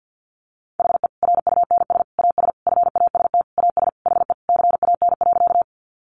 Morse code, heard as a keyed tone, about 50 wpm, 700 Hz. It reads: HE RUNS AS FAST AS HE CAN.
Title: Additional Easy Phrases 50wpm